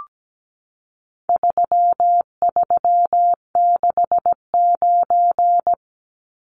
Morse code, heard as a keyed tone, about 17 wpm, 700 Hz. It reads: 3369
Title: Numbers - 4-Digit 17wpm